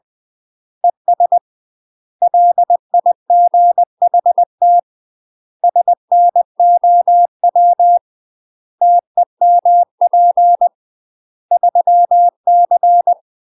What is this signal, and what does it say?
Morse code, heard as a keyed tone, about 20 wpm, 700 Hz. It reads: ES LIGHT SNOW TEMP 3C
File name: UK MASTER QSOs TRAINING / 20WPM / UK MASTER QSOs 0056